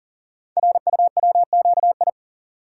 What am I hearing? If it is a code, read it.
Morse code, heard as a keyed tone, about 39 wpm, 690 Hz. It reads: RUWQI
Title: Code Groups - 5 Letters - 40wpm